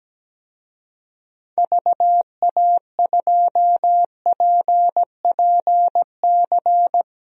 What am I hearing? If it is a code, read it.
Morse code, heard as a keyed tone, about 17 wpm, 700 Hz. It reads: VA2PPC
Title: DXCC-100 Call Signs - 6 Characters - 17wpm